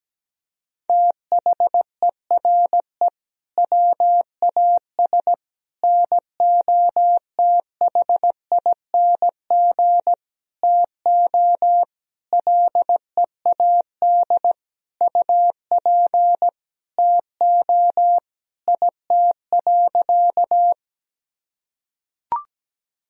Morse code, heard as a keyed tone, about 17 wpm, 700 Hz. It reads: THERE WAS NOTHING TO LEAD UP TO IT.
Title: Sentences from Top 1000 Words - Review Part 2 - 17wpm